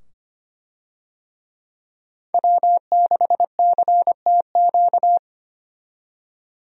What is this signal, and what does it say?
Morse code, heard as a keyed tone, about 25 wpm, 700 Hz. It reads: W6CTQ
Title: USA Call Signs - 5 Characters - 25wpm